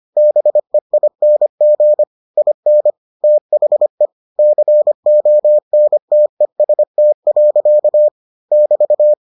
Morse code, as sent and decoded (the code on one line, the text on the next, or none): -... .. -. --. / .. -. / - .... / -.-. --- -. - . ... - .-.-.- / -...-
BEING IN THE CONTEST. <BT>